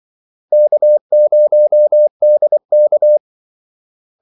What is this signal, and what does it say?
Morse code, heard as a keyed tone, about 24 wpm, 600 Hz. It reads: K0DK
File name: Callsigns-24wpm